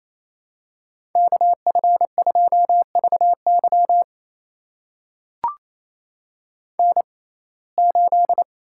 Morse code, heard as a keyed tone, about 28 wpm, spaced as for 20 wpm, 700 Hz. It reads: KF2VY N8
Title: USA Call Signs - 5 Characters - 6x Character Spacing - 28wpm